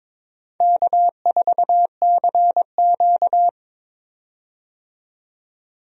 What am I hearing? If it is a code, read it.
Morse code, heard as a keyed tone, about 21 wpm, 700 Hz. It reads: K4CQ
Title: USA Call Signs - 4 Characters - 22wpm